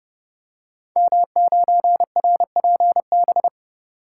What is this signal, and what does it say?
Morse code, heard as a keyed tone, about 30 wpm, 700 Hz. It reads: M9RPB
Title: Code Groups - 5 Letter:Numbers - 30wpm